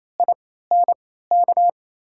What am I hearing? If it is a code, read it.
Morse code, heard as a keyed tone, about 28 wpm, spaced as for 16 wpm, 700 Hz. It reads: INK
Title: USA Call Signs - 3x Character Spacing - 28wpm